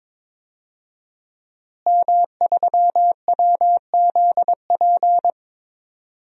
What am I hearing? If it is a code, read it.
Morse code, heard as a keyed tone, about 22 wpm, 700 Hz. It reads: M3WZP
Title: DXCC-100 Call Signs - 5 Characters - Repeated 2x - 22wpm